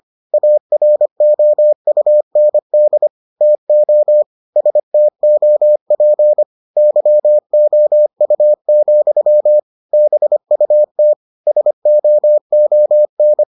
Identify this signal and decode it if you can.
Morse code, heard as a keyed tone, about 25 wpm, 600 Hz. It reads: AROUND TO STOP YOU, BUT SOON